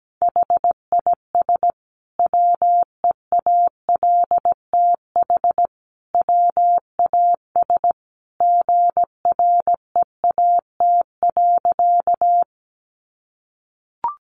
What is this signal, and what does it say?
Morse code, heard as a keyed tone, about 17 wpm, 700 Hz. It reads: HIS WEALTH WAS GREAT.
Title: Sentences from Top 1200 Words 17wpm